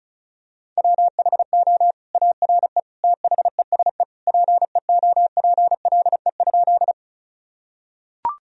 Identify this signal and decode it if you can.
Morse code, heard as a keyed tone, about 35 wpm, 700 Hz. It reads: WHO ARE THESE PEOPLE?
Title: Sentences from Top 1000 Words - Review Part 1 - 35wpm